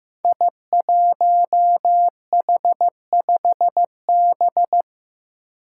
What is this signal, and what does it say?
Morse code, heard as a keyed tone, about 15 wpm, 700 Hz. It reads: I1H5B
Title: Five Challenging Letter-Number Combinations 15wpm